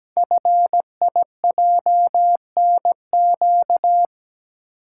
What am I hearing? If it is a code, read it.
Morse code, heard as a keyed tone, about 17 wpm, 700 Hz. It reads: FIJNQ
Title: Code Groups - 5 Letters - 17wpm